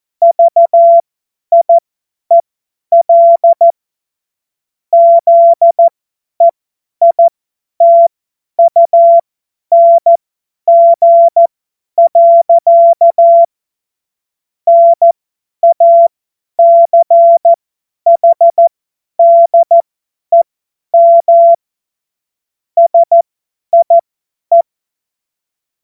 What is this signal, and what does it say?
Morse code, heard as a keyed tone, about 14 wpm, spaced as for 10 wpm, 680 Hz. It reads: VIEL ZEITUNG. NACHDEM SIE
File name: Morsekurs / Witze / Witz 11. Januar 2022, 50 F70